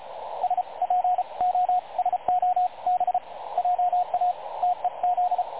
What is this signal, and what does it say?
Morse code, heard as a keyed tone, about 33 wpm, spaced as for 28 wpm, 730 Hz, with radio noise. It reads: SPOSOB JA TEZ
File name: dlazaawansowanych / nagraniazpasma / websdr recording 3565.0kHz